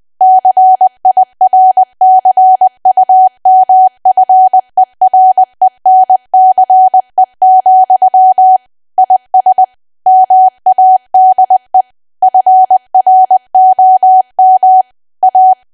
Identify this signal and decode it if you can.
Morse code, heard as a keyed tone, about 20 wpm, 750 Hz. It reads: CIRCUMFERENCE, IS MADE FROM A
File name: cwa / W1AW-Level3 / 20WPM